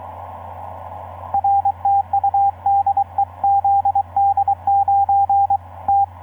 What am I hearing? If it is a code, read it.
Morse code, heard as a keyed tone, about 22 wpm, 800 Hz, with radio noise. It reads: RTU DE ZD9T
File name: zd9t f6gnz etc 15m cw 22nov2010 1145